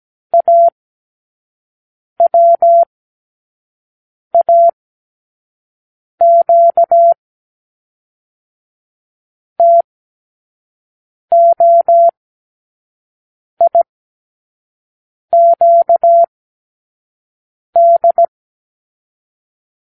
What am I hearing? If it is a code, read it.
Morse code, heard as a keyed tone, about 17 wpm, 680 Hz. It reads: A W A Q T O I Q D
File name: k7qo / Lesson 36 (A-X)